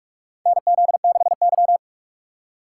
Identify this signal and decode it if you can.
Morse code, heard as a keyed tone, about 42 wpm, 700 Hz. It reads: NZ6Y
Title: USA Call Signs - 6x Character Spacing - 45wpm